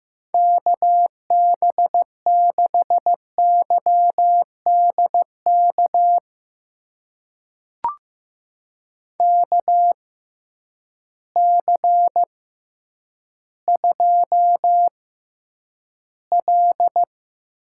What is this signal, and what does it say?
Morse code, heard as a keyed tone, about 15 wpm, spaced as for 9 wpm, 700 Hz. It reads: KB6YDK KC2L
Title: USA Call Signs - 6 Characters - 6x Character Spacing - 15wpm